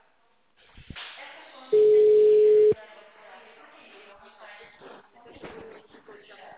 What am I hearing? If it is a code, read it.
Ringback tone (Europe, 425 Hz), with speech.
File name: out- -2039